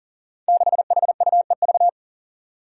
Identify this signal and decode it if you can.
Morse code, heard as a keyed tone, about 40 wpm, 700 Hz. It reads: 6HUEV
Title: Five Challenging Letter-Number Combinations 40wpm